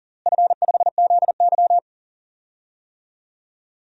Morse code, heard as a keyed tone, about 40 wpm, 700 Hz. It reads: F5ZY
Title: DXCC-100 Call Signs - 4 Characters - Repeated 2x - 40wpm